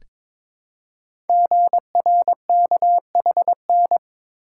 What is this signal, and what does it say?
Morse code, heard as a keyed tone, about 22 wpm, 700 Hz. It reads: GRKHN
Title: Code Groups - 5 Letter:Numbers - 22wpm